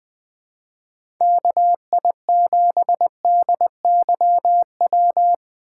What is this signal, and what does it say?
Morse code, heard as a keyed tone, about 20 wpm, 700 Hz. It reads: KI7DYW
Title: USA Call Signs - 6 Characters - 3x Character Spacing - 20wpm